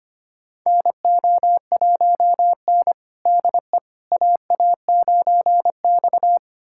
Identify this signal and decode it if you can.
Morse code, heard as a keyed tone, about 25 wpm, 700 Hz. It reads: NO1N DE AA9X
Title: USA Callsign DE Callsign - 4 Characters - 25wpm